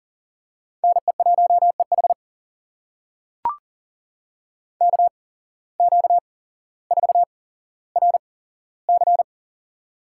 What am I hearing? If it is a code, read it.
Morse code, heard as a keyed tone, about 40 wpm, spaced as for 17 wpm, 700 Hz. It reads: NE1EH KQ4RC
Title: USA Call Signs - 5 Characters - 8x Character Spacing - 40wpm